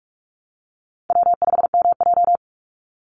Morse code, heard as a keyed tone, about 44 wpm, 700 Hz. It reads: W5MJ
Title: Easy Call Signs Repeated 2x - 45wpm